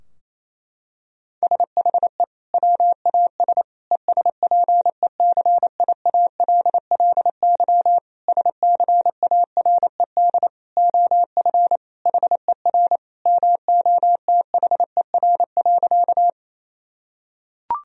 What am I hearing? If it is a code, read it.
Morse code, heard as a keyed tone, about 28 wpm, 700 Hz. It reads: SHE WAS ESPECIALLY SCARED OF HER MOTHER.
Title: Sentences from Top 1000 Words - Review Part 4 - 28wpm